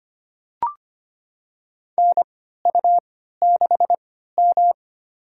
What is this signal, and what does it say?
Morse code, heard as a keyed tone, about 25 wpm, spaced as for 14 wpm, 700 Hz. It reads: NU6M
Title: USA Call Signs - 4 Characters - 3x Character Spacing - 25wpm